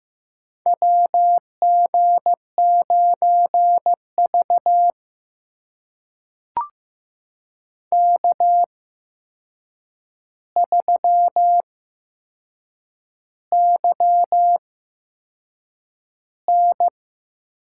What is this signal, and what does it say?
Morse code, heard as a keyed tone, about 15 wpm, spaced as for 6 wpm, 700 Hz. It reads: WG9V K3YN
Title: USA Call Signs - 4 Characters - 8x Character Spacing - 15wpm